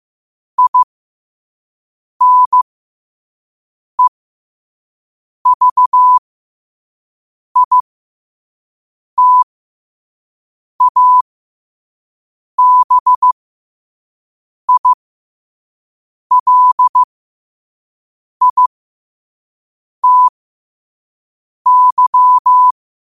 Morse code, heard as a keyed tone, about 15 wpm, spaced as for 5 wpm, 1000 Hz. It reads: INEVITABILITY
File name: qotdfiles / 5wpm